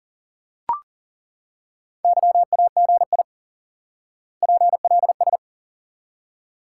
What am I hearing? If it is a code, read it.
Morse code, heard as a keyed tone, about 40 wpm, 700 Hz. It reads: YAGI PLS
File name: IQE Course 3 - Lesson 89 - YAGI - Two Elements - 40wpm